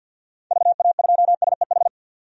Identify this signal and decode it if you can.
Morse code, heard as a keyed tone, about 52 wpm, 700 Hz. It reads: VA2SEH
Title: DXCC-100 Call Signs - 6 Characters - 50wpm